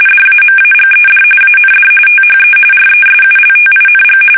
RTTY, about 45.45 baud. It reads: DAMAGED AND THE RESULTING
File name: fsk850 / usnavy3